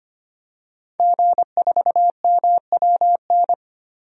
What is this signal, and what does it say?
Morse code, heard as a keyed tone, about 25 wpm, 700 Hz. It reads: G4MWN
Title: DXCC-100 Call Signs - 5 Characters - Repeated 3x - 25wpm